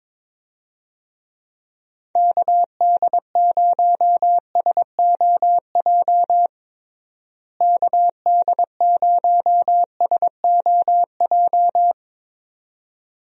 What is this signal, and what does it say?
Morse code, heard as a keyed tone, about 22 wpm, 700 Hz. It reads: KD0SOJ KD0SOJ
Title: USA Call Signs - 6 Characters - Repeated 2x - 22wpm